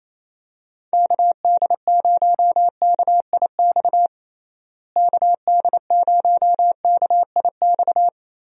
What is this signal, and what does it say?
Morse code, heard as a keyed tone, about 28 wpm, 700 Hz. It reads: KD0KIX KD0KIX
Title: USA Call Signs - 6 Characters - Repeated 3x - 28wpm